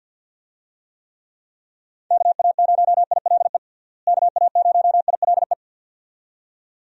Morse code, heard as a keyed tone, about 45 wpm, 700 Hz. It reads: KA0ILE KA0ILE
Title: USA Call Signs - 6 Characters - Repeated 2x - 50wpm